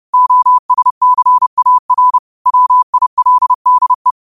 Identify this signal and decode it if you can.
Morse code, heard as a keyed tone, about 30 wpm, 1000 Hz. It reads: OSCAR WILDE
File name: qotdfiles / 30wpm